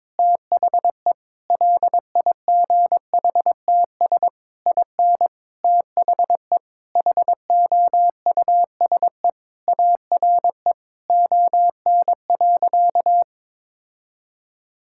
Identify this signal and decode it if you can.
Morse code, heard as a keyed tone, about 22 wpm, 700 Hz. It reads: THE LIGHTS IN THE HOUSE ARE ON.
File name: practice / 22wpm / Sentences with 2-Word Phrases - 22wpm